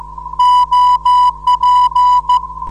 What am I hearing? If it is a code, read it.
Morse code, heard as a keyed tone, about 14 wpm, 990 Hz. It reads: OP
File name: 472khz qso web / 29march2014 / 01 dl4yhf 29mar2014 2248utc